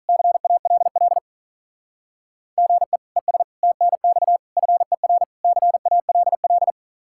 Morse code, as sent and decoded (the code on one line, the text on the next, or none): -.-. .- .-.. .-.. / --. / . ... / - -. -..- / ..-. .-. / -.-. .- .-.. .-..
CALL GE ES TNX FER CALL